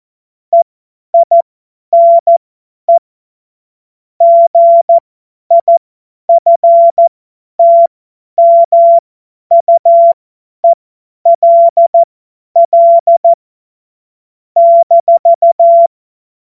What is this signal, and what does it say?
Morse code, heard as a keyed tone, about 14 wpm, spaced as for 10 wpm, 680 Hz. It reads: EINE GIFTMUELL -
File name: Morsekurs / Witze / Witz 03 Juni 2025 50 F70